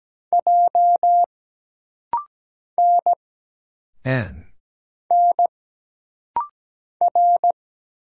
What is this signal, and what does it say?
Morse code, heard as a keyed tone, about 17 wpm, 700 Hz. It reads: J N N R
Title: Single Letters - Rapid-Fire 17wpm